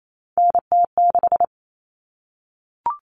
Morse code, heard as a keyed tone, about 28 wpm, 700 Hz. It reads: NT6
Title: All Cut Numbers - 3-Digit 28wpm